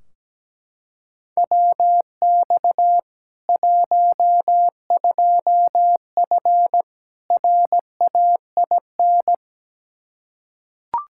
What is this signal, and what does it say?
Morse code, heard as a keyed tone, about 17 wpm, 700 Hz. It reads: WX 12F RAIN
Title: K1USN SST QSO Practice 17wpm